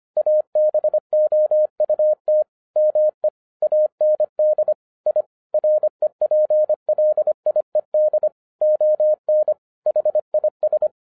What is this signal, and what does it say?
Morse code, heard as a keyed tone, about 25 wpm, 610 Hz. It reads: ABOUT ME AND I REPLIED ON HIS
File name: 2011-07-mp3 / n34 THE 3.11 2 25wpm